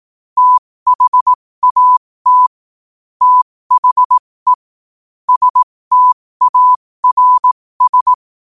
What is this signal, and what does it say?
Morse code, heard as a keyed tone, about 18 wpm, spaced as for 15 wpm, 1000 Hz. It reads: THAT THE STARS
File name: qotdfiles / 15wpm